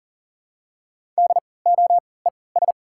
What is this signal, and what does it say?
Morse code, heard as a keyed tone, about 40 wpm, spaced as for 23 wpm, 700 Hz. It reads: DOES
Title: 3x Character Spacing - Top 200 Words - 40wpm